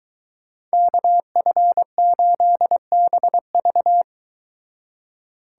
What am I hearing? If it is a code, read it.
Morse code, heard as a keyed tone, about 23 wpm, 700 Hz. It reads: KF8BV